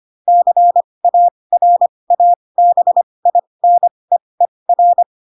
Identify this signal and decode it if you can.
Morse code, heard as a keyed tone, about 25 wpm, spaced as for 20 wpm, 700 Hz. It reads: CARABINEER